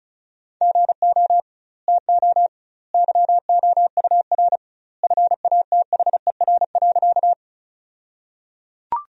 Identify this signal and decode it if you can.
Morse code, heard as a keyed tone, about 36 wpm, 700 Hz. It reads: GO TO YOUR FATHER.
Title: 2x Word-Spacing - Sentences from Top 300 Words - 35wpm